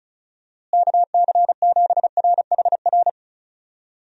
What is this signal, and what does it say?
Morse code, heard as a keyed tone, about 36 wpm, 700 Hz. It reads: KC7RHR
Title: USA Call Signs - 6 Characters - Repeated 3x - 35wpm